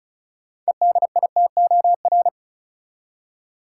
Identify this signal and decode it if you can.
Morse code, heard as a keyed tone, about 34 wpm, 700 Hz. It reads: EDITOR
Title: Six Letter Words - 35wpm